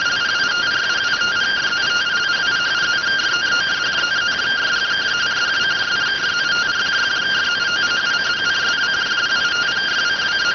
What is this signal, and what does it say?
Two-tone teleprinter signal, about 170 Hz shift.